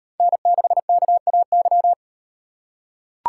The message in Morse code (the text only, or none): N6KAY